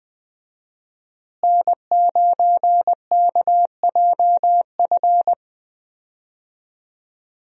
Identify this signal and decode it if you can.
Morse code, heard as a keyed tone, about 20 wpm, 700 Hz. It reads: N9KJF
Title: USA Call Signs - 5 Characters - Repeated 3x - 20wpm